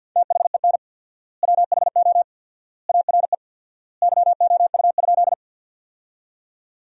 Morse code, heard as a keyed tone, about 50 wpm, 700 Hz. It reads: THEN WHO ARE YOU?
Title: 4x Word-Spacing - Sentences from Top 100 Words - 50wpm